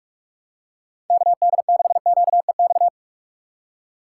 Morse code, heard as a keyed tone, about 47 wpm, 700 Hz. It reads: KD6QEX
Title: USA Call Signs - 6x Character Spacing - 45wpm